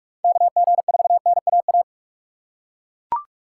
Morse code, heard as a keyed tone, about 42 wpm, 700 Hz. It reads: KG4NAA